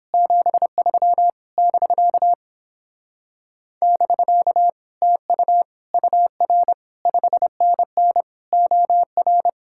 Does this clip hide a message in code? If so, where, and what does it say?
Yes: Morse code from 0.1 to 9.5 s, heard as a keyed tone, about 29 wpm, 700 Hz. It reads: 73 <BK> <BK> TU UR 5NN OR